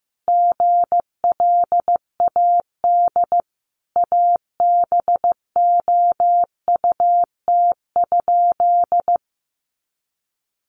Morse code, heard as a keyed tone, about 15 wpm, 700 Hz. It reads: GLAD ABOUT?
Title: Sentences from Top 800 Words 15wpm